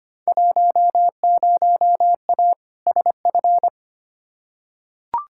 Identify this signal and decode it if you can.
Morse code, heard as a keyed tone, about 25 wpm, 700 Hz. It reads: 10A SF
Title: Field Day Exchanges Encore - 25wpm